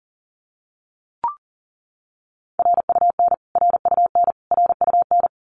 Morse code, heard as a keyed tone, about 40 wpm, 700 Hz. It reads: RUN RUN RUN
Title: Top 300 Words Repeated 40wpm